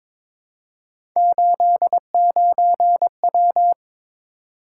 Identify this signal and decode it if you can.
Morse code, heard as a keyed tone, about 22 wpm, 700 Hz. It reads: 89W
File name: practice / 22wpm / Three Letter-Number Combinations - 22wpm